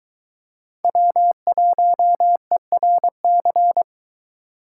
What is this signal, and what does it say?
Morse code, heard as a keyed tone, about 21 wpm, 700 Hz. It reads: W1ERC